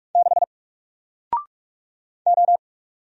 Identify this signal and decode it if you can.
Morse code, heard as a keyed tone, about 44 wpm, 700 Hz. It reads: 6 O
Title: Single Letter-Number Alt 45wpm